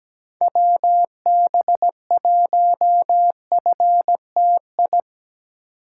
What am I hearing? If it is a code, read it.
Morse code, heard as a keyed tone, about 17 wpm, 700 Hz. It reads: WB1FTI
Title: USA Call Signs - 6 Characters - 8x Character Spacing - 17wpm